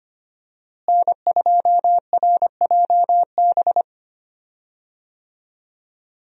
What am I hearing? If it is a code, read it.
Morse code, heard as a keyed tone, about 25 wpm, 700 Hz. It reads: N2RJB
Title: USA Call Signs - 5 Characters - Repeated 2x - 25wpm